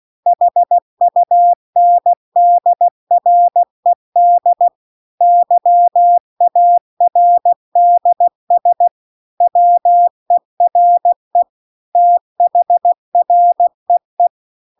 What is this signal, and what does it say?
Morse code, heard as a keyed tone, about 16 wpm, 700 Hz. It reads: HUNDRED YARDS WERE THREE
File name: 16WPM